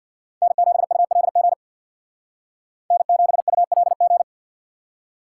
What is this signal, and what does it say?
Morse code, heard as a keyed tone, about 52 wpm, 700 Hz. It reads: N7ULG N7ULG
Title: Speed Racing - USA Call Signs 50wpm